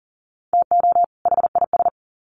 Morse code, heard as a keyed tone, about 40 wpm, 700 Hz. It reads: TO HIS
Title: Sets of 2 Words 40wpm